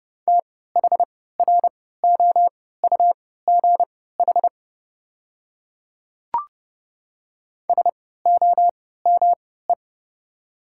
Morse code, heard as a keyed tone, about 30 wpm, spaced as for 17 wpm, 700 Hz. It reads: THROUGH SOME
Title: 3x Character Spacing - Top 200 Words - 30wpm